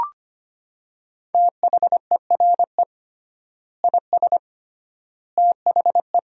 Morse code, heard as a keyed tone, about 25 wpm, 700 Hz. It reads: THERE IS THE
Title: 3x Word-Spacing - Sentences from Top 200 Words - 25wpm